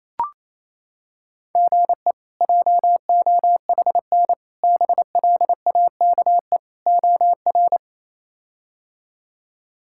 Morse code, heard as a keyed tone, about 28 wpm, 700 Hz. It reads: GE JOHN BLAKE OR
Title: K1USN SST QSO Practice - 28wpm